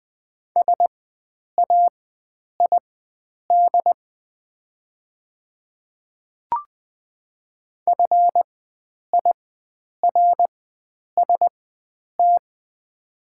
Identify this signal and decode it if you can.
Morse code, heard as a keyed tone, about 20 wpm, spaced as for 9 wpm, 690 Hz. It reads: SAID FIRST
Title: 4x Character Spacing - Top 100 Words - 20wpm